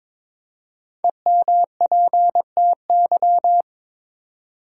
Morse code, heard as a keyed tone, about 22 wpm, 700 Hz. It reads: EMPTY
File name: 5 Letter Words - First 3 Letters Given - 22wpm